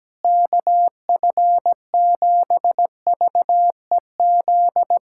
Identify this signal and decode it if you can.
Morse code, heard as a keyed tone, about 17 wpm, 700 Hz. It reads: KF7VEZ
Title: USA Callsign DE Callsign 17wpm